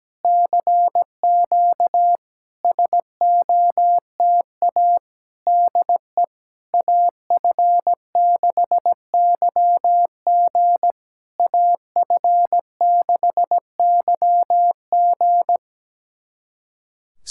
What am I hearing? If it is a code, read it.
Morse code, heard as a keyed tone, about 17 wpm, 700 Hz. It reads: CQ SOTA DE AF6YG AF6YG
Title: SOTA QSO Practice 17wpm